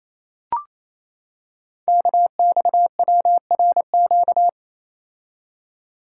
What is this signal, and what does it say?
Morse code, heard as a keyed tone, about 29 wpm, 700 Hz. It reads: KXWRQ
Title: Code Groups - 5 Letters - 28wpm